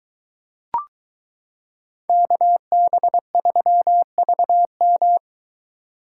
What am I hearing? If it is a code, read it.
Morse code, heard as a keyed tone, about 23 wpm, 700 Hz. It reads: KB3VM